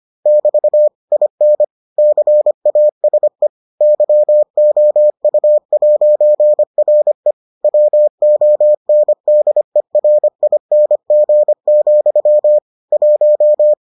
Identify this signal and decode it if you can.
Morse code, heard as a keyed tone, about 25 wpm, 600 Hz. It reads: <BT> IN CASE YOU'RE WONDERING, 1